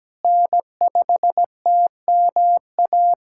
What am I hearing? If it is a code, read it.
Morse code, heard as a keyed tone, about 17 wpm, 700 Hz. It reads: N5TMA
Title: USA Callsign DE Callsign - 5 Characters - 17wpm